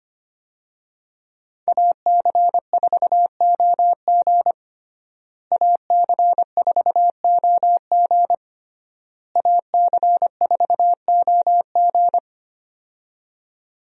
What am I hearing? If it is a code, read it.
Morse code, heard as a keyed tone, about 25 wpm, 700 Hz. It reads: AC4OG AC4OG AC4OG
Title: USA Call Signs - 5 Characters - Repeated 3x - 25wpm